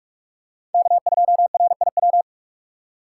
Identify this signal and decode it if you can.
Morse code, heard as a keyed tone, about 40 wpm, 700 Hz. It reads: K2RIW